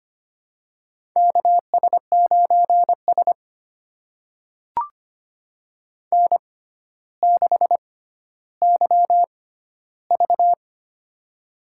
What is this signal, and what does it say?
Morse code, heard as a keyed tone, about 25 wpm, spaced as for 13 wpm, 700 Hz. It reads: KS9S N6YV
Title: USA Call Signs - 4 Characters - 6x Character Spacing - 25wpm